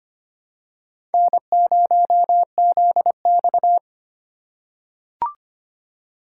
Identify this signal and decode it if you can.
Morse code, heard as a keyed tone, about 25 wpm, 700 Hz. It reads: N0ZX
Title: USA Call Signs - 4 Characters - 6x Character Spacing - 25wpm